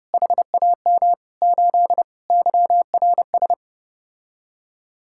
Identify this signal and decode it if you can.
Morse code, heard as a keyed tone, about 31 wpm, 700 Hz. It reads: HAM 8 YRS